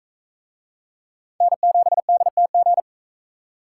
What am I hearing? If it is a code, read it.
Morse code, heard as a keyed tone, about 37 wpm, 700 Hz. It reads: N7DTG